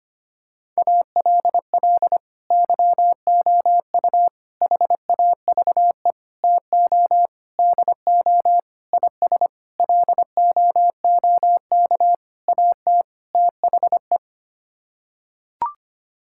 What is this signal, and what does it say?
Morse code, heard as a keyed tone, about 25 wpm, 700 Hz. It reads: ALL YOU HAVE TO DO IS LOOK AT THE
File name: Sets of 9 Words - 25wpm